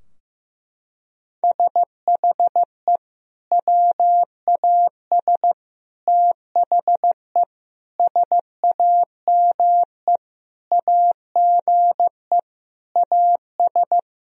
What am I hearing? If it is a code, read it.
Morse code, heard as a keyed tone, about 15 wpm, 700 Hz. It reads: SHE WAS THE SAME AGE AS
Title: Sentences from Top 1000 Words - Review Part 3 - 15wpm